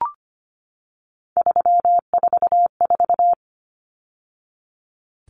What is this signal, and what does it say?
Morse code, heard as a keyed tone, about 25 wpm, 700 Hz. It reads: 344
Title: Numbers - 3-Digit 25wpm